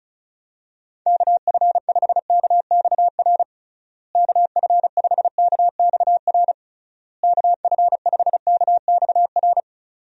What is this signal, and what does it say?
Morse code, heard as a keyed tone, about 34 wpm, 700 Hz. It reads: KF5KXR KF5KXR KF5KXR
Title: USA Call Signs - 6 Characters - Repeated 3x - 35wpm